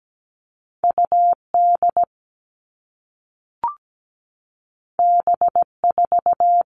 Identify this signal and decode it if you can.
Morse code, heard as a keyed tone, about 17 wpm, 700 Hz. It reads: UD B4
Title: All Cut Numbers - 2-Digit 17wpm